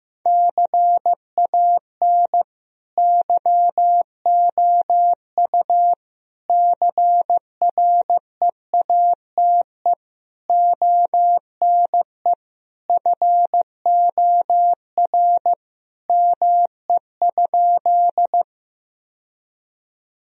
Morse code, heard as a keyed tone, about 15 wpm, 700 Hz. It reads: CAN YOU CREATE ONE FOR ME?
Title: Sentences from Top 1000 Words - Review Part 3 - 15wpm